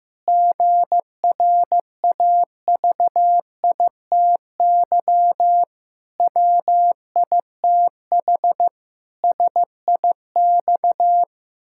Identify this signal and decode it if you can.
Morse code, heard as a keyed tone, about 14 wpm, 700 Hz. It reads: GRAVITY WITH SIX
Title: Pangram Challenge v2 - 15wpm